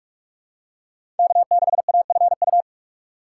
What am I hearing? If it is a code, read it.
Morse code, heard as a keyed tone, about 43 wpm, 700 Hz. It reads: K6AFU
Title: USA Call Signs - 5 Characters - 4x Character Spacing - 45wpm